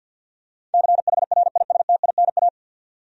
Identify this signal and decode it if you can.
Morse code, heard as a keyed tone, about 53 wpm, 700 Hz. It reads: CHRISTINA
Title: USA Common Names - 50wpm